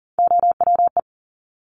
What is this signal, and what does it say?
Morse code, heard as a keyed tone, about 39 wpm, 700 Hz. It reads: OWE
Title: Top Three Letter Combinations in Words 40wpm